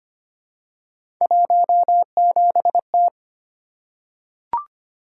Morse code, heard as a keyed tone, about 25 wpm, 700 Hz. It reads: 17T
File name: Three Letter-Number Combinations - 25wpm